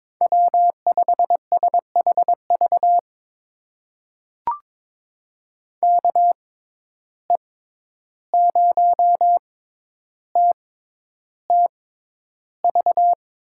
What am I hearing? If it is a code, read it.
Morse code, heard as a keyed tone, about 22 wpm, spaced as for 11 wpm, 700 Hz. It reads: W5SHV KE0TTV
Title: USA Call Signs - 6x Character Spacing - 22wpm